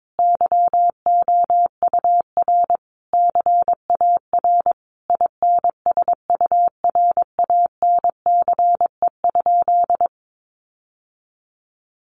Morse code, heard as a keyed tone, about 22 wpm, 700 Hz. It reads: YOUR CAR INSURANCE?
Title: Sentences from Top 1100 Words 22wpm